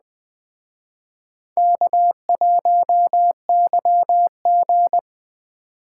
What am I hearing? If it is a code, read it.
Morse code, heard as a keyed tone, about 20 wpm, 700 Hz. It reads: K1YG
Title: Speed Racing - USA 1x2 Call Signs 17wpm